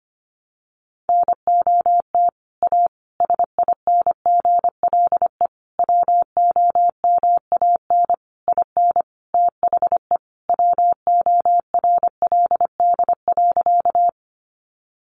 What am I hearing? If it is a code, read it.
Morse code, heard as a keyed tone, about 25 wpm, 700 Hz. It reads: NOT A SINGLE WOMAN IN THE WORLD.
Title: Sentences from Top 1100 Words 25wpm